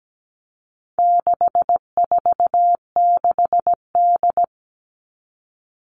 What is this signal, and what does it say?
Morse code, heard as a keyed tone, about 17 wpm, 700 Hz. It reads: 646D
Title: All Cut Numbers - 4-Digit 17wpm